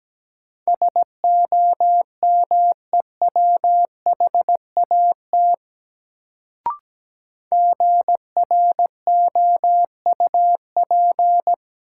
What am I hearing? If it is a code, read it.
Morse code, heard as a keyed tone, about 17 wpm, 700 Hz. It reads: SOMEWHAT GROUP
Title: Sentences from Top 1000 Words - Review Part 1 - New Word Drill - 17wpm